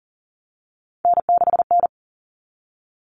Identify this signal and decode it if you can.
Morse code, heard as a keyed tone, about 40 wpm, 700 Hz. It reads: N6N
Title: All Cut Numbers - 3-Digit 40wpm